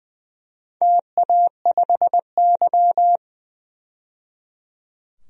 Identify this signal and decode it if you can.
Morse code, heard as a keyed tone, about 20 wpm, 700 Hz. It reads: TA5Y
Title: DXCC-100 Call Signs - 4 Characters - 20wpm